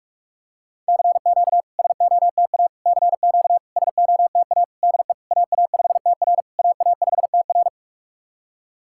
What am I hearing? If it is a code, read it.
Morse code, heard as a keyed tone, about 47 wpm, 700 Hz. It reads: CQ SOTA CQ SOTA DE AA5TR AA5TR
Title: SOTA QSO Practice 45wpm